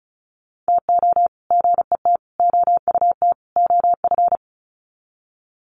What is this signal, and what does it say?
Morse code, heard as a keyed tone, about 34 wpm, 700 Hz. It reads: TO GET OUT OF
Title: Sets of 4 Words 35wpm